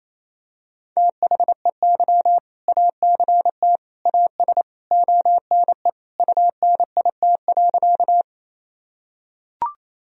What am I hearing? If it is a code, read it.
Morse code, heard as a keyed tone, about 27 wpm, 700 Hz. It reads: THEY ACT AS ONE UNIT.
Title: Sentences from Top 1000 Words - Review Part 2 - 28wpm